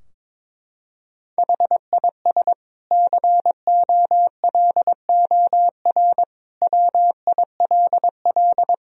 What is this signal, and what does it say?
Morse code, heard as a keyed tone, about 22 wpm, 700 Hz. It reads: HIS COLOR WILL
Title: Sentences from Top 1000 Words - Review Part 2 - 22wpm